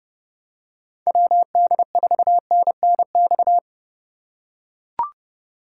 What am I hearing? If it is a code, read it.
Morse code, heard as a keyed tone, about 29 wpm, 700 Hz. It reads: WD4NNX
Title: USA Call Signs - 3x Character Spacing - 30wpm